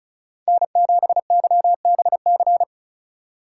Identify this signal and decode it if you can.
Morse code, heard as a keyed tone, about 36 wpm, 700 Hz. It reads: N7YBC
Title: USA Call Signs - 5 Characters - 3x Character Spacing - 35wpm